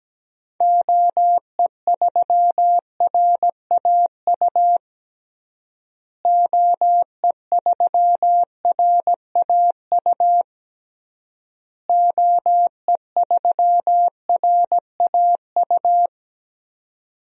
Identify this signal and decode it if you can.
Morse code, heard as a keyed tone, about 17 wpm, 700 Hz. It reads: OE3RAU OE3RAU OE3RAU
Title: DXCC-100 Call Signs - 6 Characters - Repeated 3x - 17wpm